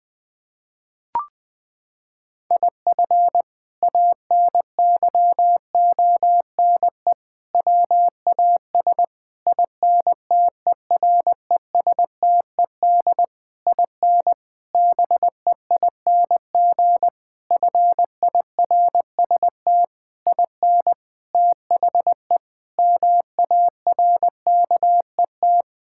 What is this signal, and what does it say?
Morse code, heard as a keyed tone, about 21 wpm, 700 Hz. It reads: IF ANYONE WAS INTERESTED IN BEING FIRST IN THE MARKET
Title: Sentences from Top 1600 Words 20wpm